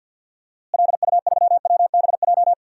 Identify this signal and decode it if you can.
Morse code, heard as a keyed tone, about 51 wpm, 700 Hz. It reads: LU3WBJ
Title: DXCC-100 Call Signs - 6 Characters - Repeated 3x - 50wpm